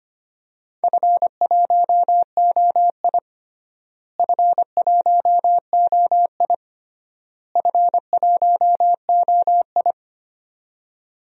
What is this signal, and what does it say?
Morse code, heard as a keyed tone, about 25 wpm, 700 Hz. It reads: F1OI F1OI F1OI
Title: DXCC-100 Call Signs - 4 Characters - Repeated 3x - 25wpm